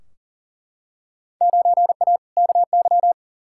Morse code, heard as a keyed tone, about 40 wpm, 700 Hz. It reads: 9A KY
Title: Field Day Exchanges Encore - 40wpm